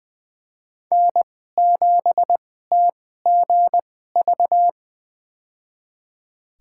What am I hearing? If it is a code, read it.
Morse code, heard as a keyed tone, about 21 wpm, spaced as for 15 wpm, 700 Hz. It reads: N7TGV
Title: USA Call Signs - 2x Character Spacing - 20wpm